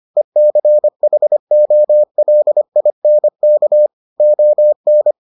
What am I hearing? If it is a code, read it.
Morse code, heard as a keyed tone, about 25 wpm, 600 Hz. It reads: ECHOLINK ON